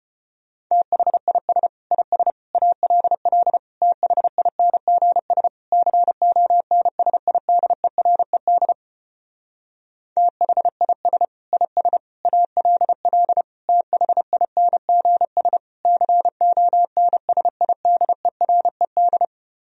Morse code, heard as a keyed tone, about 33 wpm, 700 Hz. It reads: THIS IS ALL THINGS CONSIDERED THIS IS ALL THINGS CONSIDERED